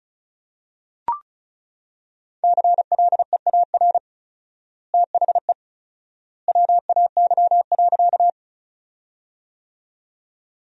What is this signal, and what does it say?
Morse code, heard as a keyed tone, about 34 wpm, 700 Hz. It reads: CLEAR THE WAY.
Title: 4x Word-Spacing - Sentences from Top 500 Words - 35wpm